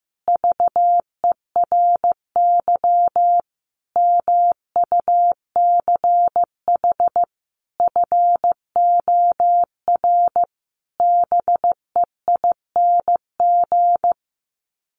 Morse code, heard as a keyed tone, about 14 wpm, 700 Hz. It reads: VERY MUCH FOR BEING
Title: Sets of 5 Words Encore 15wpm